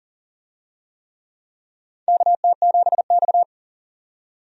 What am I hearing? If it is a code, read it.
Morse code, heard as a keyed tone, about 40 wpm, 700 Hz. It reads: KT7X
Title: USA Call Signs - 4 Characters - 40wpm